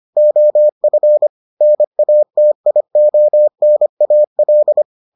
Morse code, heard as a keyed tone, about 25 wpm, 600 Hz. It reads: OF NATIONAL